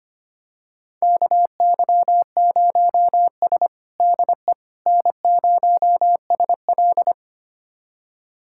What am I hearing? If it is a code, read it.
Morse code, heard as a keyed tone, about 25 wpm, 700 Hz. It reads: KY0S DE N0SL
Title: USA Callsign DE Callsign - 4 Characters - 25wpm